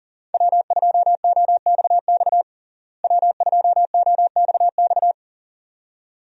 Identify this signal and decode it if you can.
Morse code, heard as a keyed tone, about 37 wpm, 700 Hz. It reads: W2OXX W2OXX
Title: USA Call Signs - 5 Characters - Repeated 2x - 40wpm